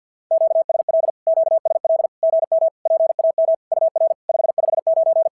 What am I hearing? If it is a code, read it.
Morse code, heard as a keyed tone, about 50 wpm, 650 Hz. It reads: QSL QSL GM PAM UR 559
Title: SOTA QSO Practice 50wpm